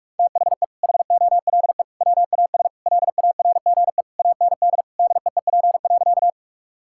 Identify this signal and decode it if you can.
Morse code, heard as a keyed tone, about 42 wpm, 700 Hz. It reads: THE HOLE WAS LARGE AND DEEP.
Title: Sentences from Top 1600 Words 45wpm